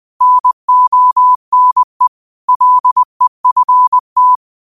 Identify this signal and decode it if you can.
Morse code, heard as a keyed tone, about 20 wpm, 1000 Hz. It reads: NONE LEFT
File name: qotdfiles / 20wpm